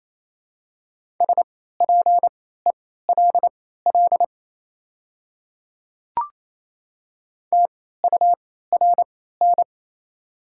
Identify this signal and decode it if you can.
Morse code, heard as a keyed tone, about 27 wpm, spaced as for 16 wpm, 700 Hz. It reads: SPELL TURN
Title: 3x Character Spacing - Top 200 Words - 28wpm